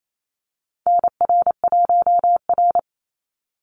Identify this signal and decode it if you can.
Morse code, heard as a keyed tone, about 28 wpm, 700 Hz. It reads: NR1R
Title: Call Signs Repeated 2x - 28wpm